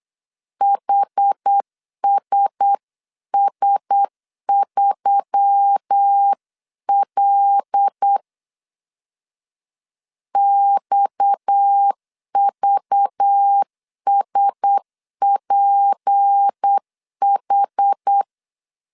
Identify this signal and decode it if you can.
Morse code, heard as a keyed tone, about 8 wpm, 800 Hz. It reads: HSS3L XVSPH
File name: CW / DET / 1 6 1500C S B 8wpm 5142